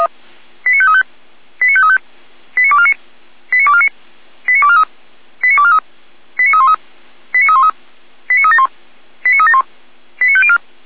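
Five-tone selective call (ZVEI), heard as quick runs of five tones, tones of about 70 ms. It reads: 89535 89535 89249 89249 89243 89243 89313 89313 89371 89371 89504
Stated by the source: Five-tone ZVEI selective call 89277, received by SDR.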